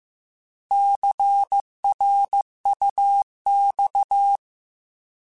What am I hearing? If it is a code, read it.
Morse code, heard as a keyed tone, about 15 wpm, 800 Hz. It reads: CRUX